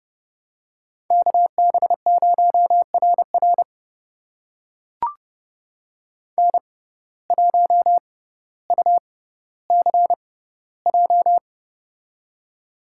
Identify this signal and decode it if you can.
Morse code, heard as a keyed tone, about 31 wpm, spaced as for 16 wpm, 700 Hz. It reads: KB0RR N1UCJ
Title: USA Call Signs - 5 Characters - 6x Character Spacing - 30wpm